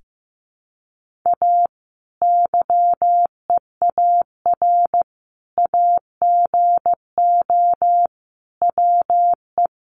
Morse code, heard as a keyed tone, about 14 wpm, 700 Hz. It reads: A YEAR AGO WE
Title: Sentences from Top 1400 Words 15wpm